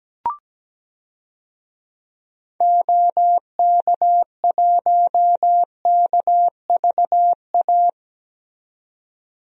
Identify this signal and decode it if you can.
Morse code, heard as a keyed tone, about 17 wpm, 700 Hz. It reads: OK1KVA
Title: DXCC-100 Call Signs - 6 Characters - 17wpm